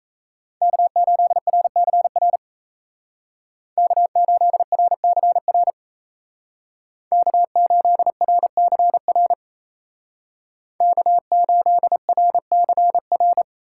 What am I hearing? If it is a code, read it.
Morse code, heard as a keyed tone, about 33 wpm, 700 Hz. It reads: K8RCR K8RCR K8RCR K8RCR
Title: Speed Racing - USA 1x3 Call Signs 28wpm